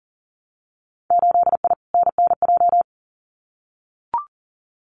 Morse code, heard as a keyed tone, about 39 wpm, 700 Hz. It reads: 8I NNJ
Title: Winter Field Day Exchanges Encore 40wpm